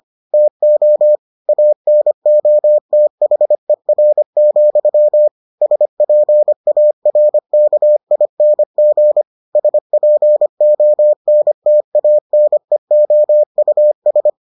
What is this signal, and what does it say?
Morse code, heard as a keyed tone, about 25 wpm, 600 Hz. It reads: TO ANOTHER, SPARKING SPONTANEOUS